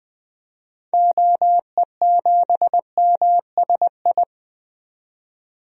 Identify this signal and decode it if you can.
Morse code, heard as a keyed tone, about 21 wpm, 700 Hz. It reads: OE7MSI
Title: DXCC-100 Call Signs - 6 Characters - 20wpm